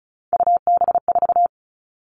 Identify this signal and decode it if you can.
Morse code, heard as a keyed tone, about 34 wpm, 700 Hz. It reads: UB4
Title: All Cut Numbers - 3-Digit 35wpm